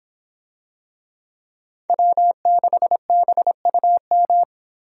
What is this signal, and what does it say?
Morse code, heard as a keyed tone, about 26 wpm, 700 Hz. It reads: W6BUM